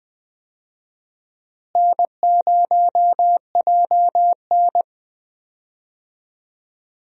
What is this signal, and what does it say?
Morse code, heard as a keyed tone, about 20 wpm, 700 Hz. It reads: N0JN
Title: USA Call Signs - 4 Characters - Repeated 2x - 20wpm